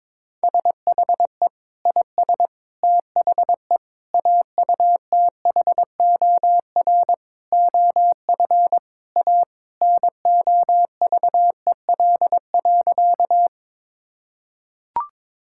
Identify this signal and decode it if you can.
Morse code, heard as a keyed tone, about 22 wpm, 700 Hz. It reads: SHE IS THE AUTHOR OF A NOVEL.
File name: practice / 22wpm / Sentences with 4-Word Phrases - 22wpm